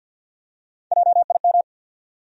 Morse code, heard as a keyed tone, about 49 wpm, 700 Hz. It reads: JIM
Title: USA Common Names - Max 3 Letters - 50wpm